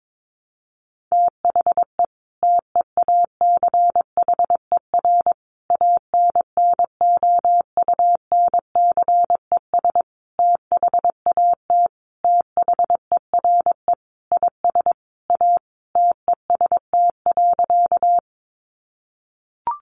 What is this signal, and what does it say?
Morse code, heard as a keyed tone, about 22 wpm, 700 Hz. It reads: THE TEACHER ANNOUNCES THAT THERE IS A TEST.
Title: Additional Easy Phrases 22wpm